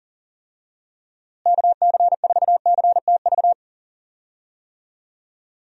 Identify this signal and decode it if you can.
Morse code, heard as a keyed tone, about 40 wpm, 700 Hz. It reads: KC4CTV
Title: USA Call Signs - 6 Characters - Repeated 2x - 40wpm